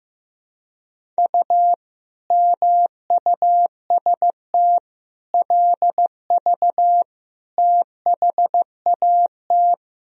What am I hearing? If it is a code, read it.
Morse code, heard as a keyed tone, about 15 wpm, 700 Hz. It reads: U MUST LV THAT
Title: Sentences from Top 400 Words and Abbreviations 15wpm